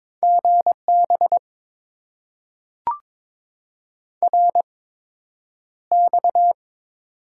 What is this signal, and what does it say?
Morse code, heard as a keyed tone, about 22 wpm, spaced as for 9 wpm, 700 Hz. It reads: GB RX
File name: Top 90 QSO Elements - 8x Character Spacing - 22wpm